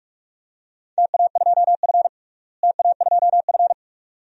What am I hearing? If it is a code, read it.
Morse code, heard as a keyed tone, about 41 wpm, 700 Hz. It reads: TA2F TA2F
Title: DXCC-100 Call Signs - 4 Characters - Repeated 2x - 45wpm